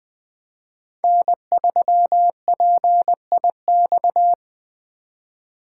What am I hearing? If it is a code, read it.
Morse code, heard as a keyed tone, about 20 wpm, 700 Hz. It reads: N3PIX
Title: USA Call Signs - 5 Characters - 8x Character Spacing - 20wpm